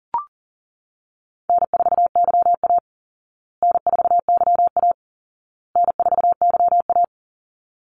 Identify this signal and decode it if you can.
Morse code, heard as a keyed tone, about 40 wpm, 700 Hz. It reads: N4YA N4YA N4YA
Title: Easy Call Signs Repeated 3x - 40wpm